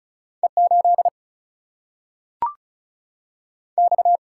Morse code, heard as a keyed tone, about 36 wpm, 700 Hz. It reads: E 8 X
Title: Single-Letter Single-Number 35wpm